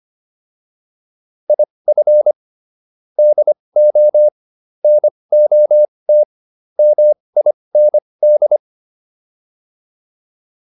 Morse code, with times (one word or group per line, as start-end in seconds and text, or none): 1.5-2.3 IF
3.2-4.3 DO
4.8-6.2 NOT
6.8-8.6 MIND